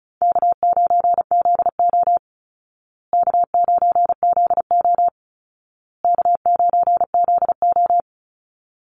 Morse code, heard as a keyed tone, about 34 wpm, 700 Hz. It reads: K9ZO K9ZO K9ZO
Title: Easy Call Signs Repeated 3x - 35wpm